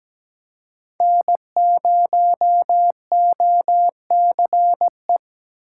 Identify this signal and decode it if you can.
Morse code, heard as a keyed tone, about 17 wpm, 700 Hz. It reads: N0OCE
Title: USA Call Signs - 5 Characters - 6x Character Spacing - 17wpm